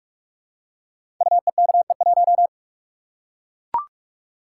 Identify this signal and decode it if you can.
Morse code, heard as a keyed tone, about 44 wpm, 700 Hz. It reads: UEKE1
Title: Code Groups - 5 Letter:Numbers - 45wpm